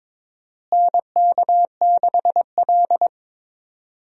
Morse code, heard as a keyed tone, about 22 wpm, 700 Hz. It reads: NK6L
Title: USA Call Signs - 4 Characters - Repeated 2x - 22wpm